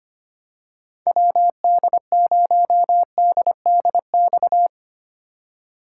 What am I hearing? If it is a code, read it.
Morse code, heard as a keyed tone, about 25 wpm, 700 Hz. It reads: WD0DDX
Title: USA Call Signs - 6 Characters - 4x Character Spacing - 25wpm